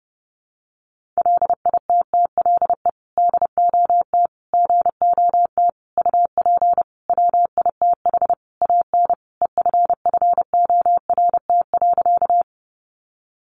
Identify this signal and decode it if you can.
Morse code, heard as a keyed tone, about 30 wpm, 700 Hz. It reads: LITTLE DOT GOT UP WITH AN EFFORT.
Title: Sentences from Top 1500 Words 30wpm